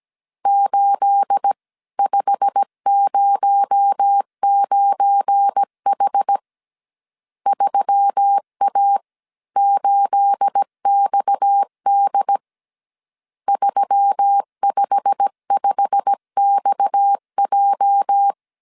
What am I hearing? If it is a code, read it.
Morse code, heard as a keyed tone, about 17 wpm, 800 Hz. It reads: 8509H 3A8XD 355XJ
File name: CW / DET / 1 9 1500C S C 16wpm 2909